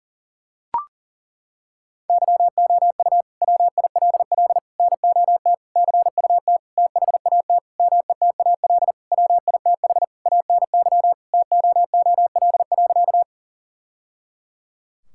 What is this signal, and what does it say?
Morse code, heard as a keyed tone, about 40 wpm, 700 Hz. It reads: YOU WILL NOT CUT THAT METAL WITH ANY TOOL.
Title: Sentences from Top 1000 Words - Review Part 6 - 40wpm